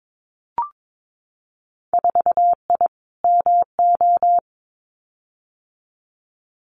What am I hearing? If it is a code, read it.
Morse code, heard as a keyed tone, about 22 wpm, 700 Hz. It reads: 4I MO
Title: Winter Field Day Exchanges 22wpm